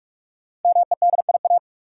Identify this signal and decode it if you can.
Morse code, heard as a keyed tone, about 44 wpm, 700 Hz. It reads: MEDIA
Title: Sentences from Top 1000 Words - Review Part 4 - New Word Drill - 45wpm